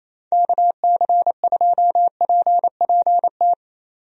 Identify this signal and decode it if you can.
Morse code, heard as a keyed tone, about 28 wpm, 700 Hz. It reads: KC2PPT
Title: USA Call Signs - 6 Characters - 28wpm